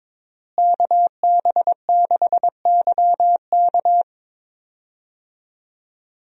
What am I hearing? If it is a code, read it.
Morse code, heard as a keyed tone, about 22 wpm, 700 Hz. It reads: KB6YK
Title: USA Call Signs - 5 Characters - Repeated 3x - 22wpm